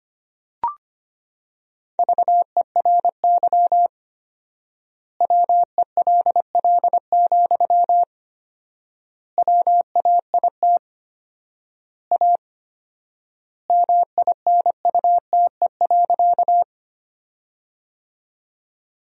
Morse code, heard as a keyed tone, about 25 wpm, 700 Hz. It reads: VERY WELL, WAIT A MINUTE.
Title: 4x Word-Spacing - Sentences from Top 500 Words - 25wpm